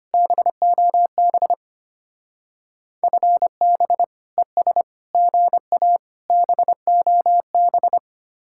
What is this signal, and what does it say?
Morse code, heard as a keyed tone, about 33 wpm, 700 Hz. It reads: BOB FB ES GA BOB